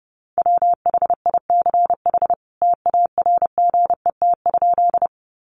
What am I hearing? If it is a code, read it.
Morse code, heard as a keyed tone, about 30 wpm, 700 Hz. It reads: WHICH TARGET?
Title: Sentences from Top 1200 Words 30wpm